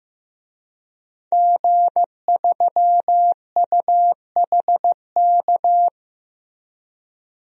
Morse code, heard as a keyed tone, about 15 wpm, 700 Hz. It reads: G3UHK
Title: DXCC-100 Call Signs - 5 Characters - Repeated 3x - 15wpm